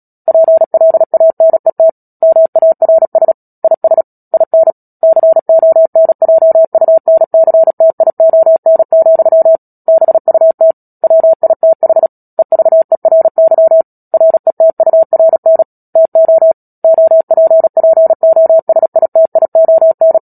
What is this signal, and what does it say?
Morse code, heard as a keyed tone, about 37 wpm, 650 Hz. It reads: PLANET MARS IS IN CONJUNCTION, BUT WITH EVERY RETURN TO OPPOSITION
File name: Book Two / 10-Chapter 10 - Epilogue - 36 WPM